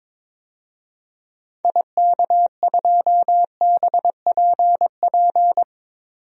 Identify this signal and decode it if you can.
Morse code, heard as a keyed tone, about 22 wpm, 700 Hz. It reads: IK2BPP
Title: DXCC-100 Call Signs - 6 Characters - 22wpm